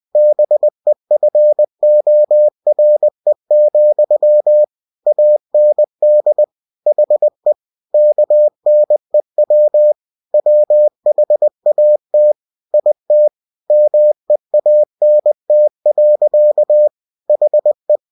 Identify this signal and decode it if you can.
Morse code, heard as a keyed tone, about 20 wpm, 600 Hz. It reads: BEFORE, AND HE KNEW WHAT IT MEANT. HE
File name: cw-books / winnie-the-pooh / winnie-20wpm-page-0039